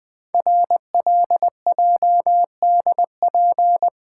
Morse code, heard as a keyed tone, about 20 wpm, 700 Hz. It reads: RLJDP
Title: Code Groups - 5 Letters - 20wpm